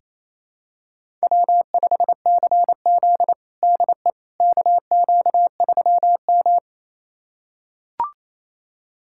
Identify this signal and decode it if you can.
Morse code, heard as a keyed tone, about 29 wpm, 700 Hz. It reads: W5CZ DE KQ3M
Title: USA Callsign DE Callsign - 4 Characters - 28wpm